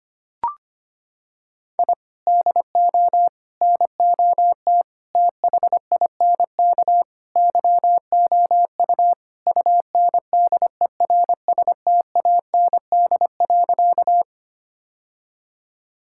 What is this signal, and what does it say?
Morse code, heard as a keyed tone, about 25 wpm, 700 Hz. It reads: I DO NOT THINK YOU UNDERSTAND.
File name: practice / 25wpm / Sentences with 4-Word Phrases - 25wpm